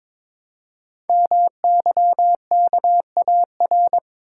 Morse code, heard as a keyed tone, about 23 wpm, 700 Hz. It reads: MYKAR
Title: Code Groups - 5 Letters - 22wpm